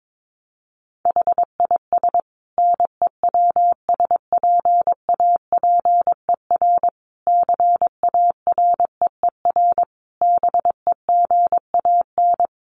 Morse code, heard as a keyed tone, about 22 wpm, 700 Hz. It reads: HIS NEWSPAPER CAREER BEGAN
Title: Sentences from Top 1100 Words 22wpm